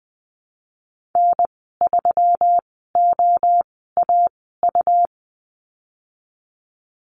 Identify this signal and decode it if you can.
Morse code, heard as a keyed tone, about 20 wpm, spaced as for 15 wpm, 700 Hz. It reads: N3OAU
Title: 128 Lesson 21g - Number 6 - Call Signs 20wpm